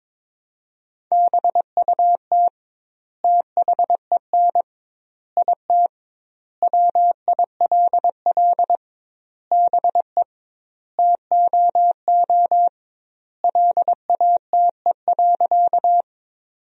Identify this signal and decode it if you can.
Morse code, heard as a keyed tone, about 22 wpm, 700 Hz. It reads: BUT THEN IT WILL BE TOO LATE.
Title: 2x Word-Spacing - Sentences from Top 300 Words - 22wpm